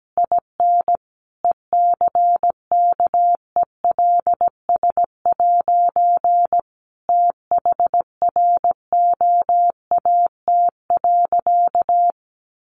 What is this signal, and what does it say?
Morse code, heard as a keyed tone, about 17 wpm, 700 Hz. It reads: IN ECKELS' THROAT.
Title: A Sound Of Thunder 17wpm